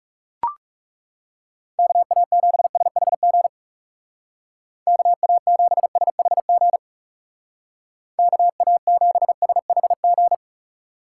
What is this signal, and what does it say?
Morse code, heard as a keyed tone, about 40 wpm, 700 Hz. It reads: KA7SHG KA7SHG KA7SHG